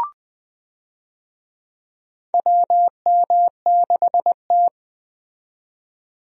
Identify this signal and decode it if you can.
Morse code, heard as a keyed tone, about 20 wpm, 700 Hz. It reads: WM6T
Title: USA Call Signs - 4 Characters - 20wpm